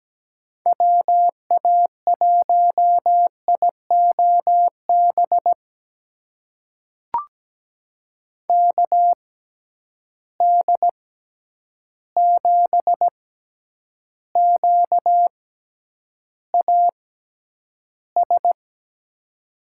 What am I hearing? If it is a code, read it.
Morse code, heard as a keyed tone, about 17 wpm, spaced as for 9 wpm, 700 Hz. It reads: WA1IOB KD7QAS
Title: USA Call Signs - 6 Characters - 6x Character Spacing - 17wpm